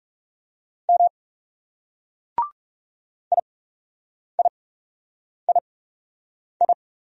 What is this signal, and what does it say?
Morse code, heard as a keyed tone, about 38 wpm, spaced as for 8 wpm, 700 Hz. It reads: M IIII